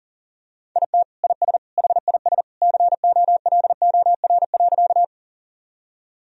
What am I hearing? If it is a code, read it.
Morse code, heard as a keyed tone, about 37 wpm, 700 Hz. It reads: IT IS HIS COLOR.
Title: Sentences from Top 1000 Words - Review Part 2 - 40wpm